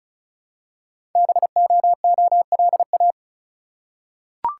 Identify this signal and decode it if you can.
Morse code, heard as a keyed tone, about 36 wpm, 700 Hz. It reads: BOOLA
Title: Code Groups - 5 Letters - 35wpm